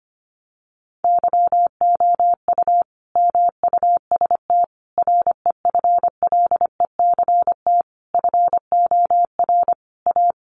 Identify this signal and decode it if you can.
Morse code, heard as a keyed tone, about 25 wpm, 700 Hz. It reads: YOU MUST REFLECT FOR A
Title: Sentences from Top 1500 Words 25wpm